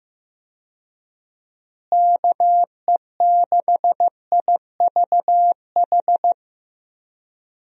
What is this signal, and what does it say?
Morse code, heard as a keyed tone, about 15 wpm, 700 Hz. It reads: KE6IVH
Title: USA Call Signs - 6 Characters - Repeated 2x - 15wpm